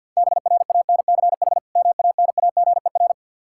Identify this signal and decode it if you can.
Morse code, heard as a keyed tone, about 47 wpm, 700 Hz. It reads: BRANCH MANAGER
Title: Practice QSO - Exchange 3 - More Info - 50wpm